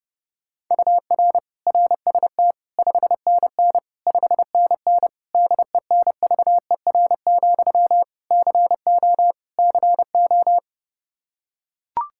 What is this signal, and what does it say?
Morse code, heard as a keyed tone, about 30 wpm, 700 Hz. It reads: UR RST 5NN 5NN DENVER, CO CO
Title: Practice QSO - Exchange 1 - Beginning - 30wpm